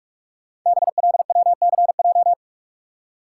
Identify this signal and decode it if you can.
Morse code, heard as a keyed tone, about 44 wpm, 700 Hz. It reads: BLWCJ
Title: Code Groups - 5 Letters - 45wpm